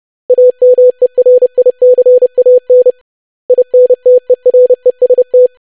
Morse code, heard as a keyed tone, about 30 wpm, 500 Hz. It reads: AMERICAN INTEREST
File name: lesson / web / basic500-25wpm